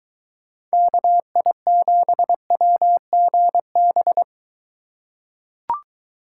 Morse code, heard as a keyed tone, about 23 wpm, 700 Hz. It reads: KI7WGB